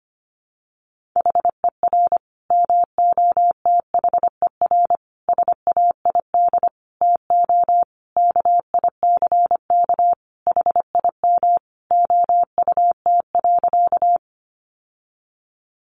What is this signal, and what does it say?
Morse code, heard as a keyed tone, about 25 wpm, 700 Hz. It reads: HER MOTHER SAID TO KICK HIM OUT.
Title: Sentences from Top 1300 Words 25wpm